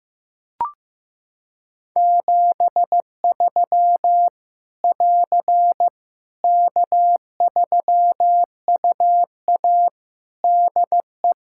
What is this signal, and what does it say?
Morse code, heard as a keyed tone, about 15 wpm, 700 Hz. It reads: 73 <AR> K3UA DE
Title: Practice QSO - Exchange 4 - Ending - 15wpm